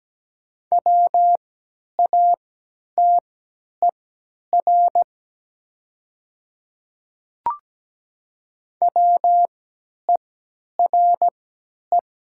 Morse code, heard as a keyed tone, about 17 wpm, spaced as for 10 wpm, 700 Hz. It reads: WATER WERE
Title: 3x Character Spacing - Top 100 Words - 17wpm